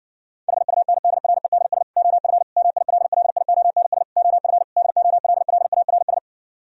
Morse code, heard as a keyed tone, about 54 wpm, 700 Hz. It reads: HUNDREDS OF MILLIONS OF DOLLARS